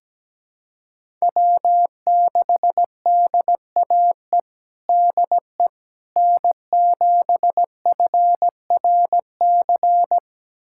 Morse code, heard as a keyed tone, about 17 wpm, 700 Hz. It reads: W6DAE DE N7FRC
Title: USA Callsign DE Callsign - 5 Characters - 17wpm